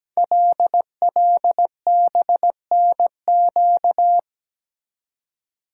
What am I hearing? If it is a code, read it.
Morse code, heard as a keyed tone, about 17 wpm, 700 Hz. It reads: LLBNQ
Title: Code Groups - 5 Letters - 17wpm